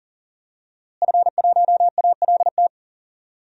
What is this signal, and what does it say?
Morse code, heard as a keyed tone, about 40 wpm, 700 Hz. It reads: F1ALT
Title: DXCC-100 Call Signs - 5 Characters - Repeated 3x - 40wpm